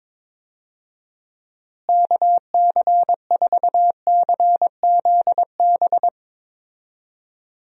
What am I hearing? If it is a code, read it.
Morse code, heard as a keyed tone, about 21 wpm, 700 Hz. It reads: KC4CZB
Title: USA Call Signs - 6 Characters - Repeated 2x - 22wpm